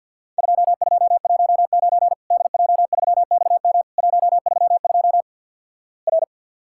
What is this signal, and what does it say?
Morse code, heard as a keyed tone, about 50 wpm, 700 Hz. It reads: 2219 DJ3XM 132 R
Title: Worked All Europe DX Contest - QTC Exchange - 50wpm